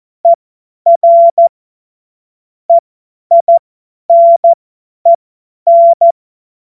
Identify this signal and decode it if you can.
Morse code, heard as a keyed tone, about 14 wpm, spaced as for 10 wpm, 680 Hz. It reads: ER EINEN
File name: Morsekurs / Witze / Witz 13 Februar 2024 50 F70